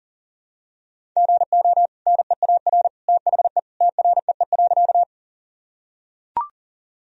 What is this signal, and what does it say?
Morse code, heard as a keyed tone, about 40 wpm, 700 Hz. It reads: GO NEAR THE TREE.
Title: Sentences from Top 1000 Words - Review Part 1 - 40wpm